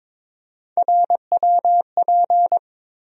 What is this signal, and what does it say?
Morse code, heard as a keyed tone, about 22 wpm, 700 Hz. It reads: RWP
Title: USA Call Sign Postfixes - 3 Letters 22wpm